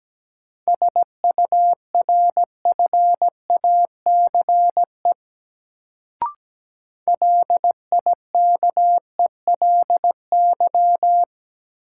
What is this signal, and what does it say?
Morse code, heard as a keyed tone, about 17 wpm, 700 Hz. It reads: SURFACE LIKELY
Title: Sentences from Top 600 Words - New Word Drill - 17wpm